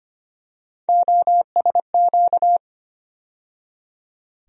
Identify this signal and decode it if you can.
Morse code, heard as a keyed tone, about 25 wpm, 700 Hz. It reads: OSQ
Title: USA Call Sign Postfixes - 3 Letters 25wpm